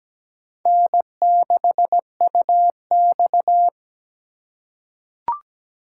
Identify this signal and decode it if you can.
Morse code, heard as a keyed tone, about 17 wpm, 700 Hz. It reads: N6UX
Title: USA Call Signs - 4 Characters - 4x Character Spacing - 17wpm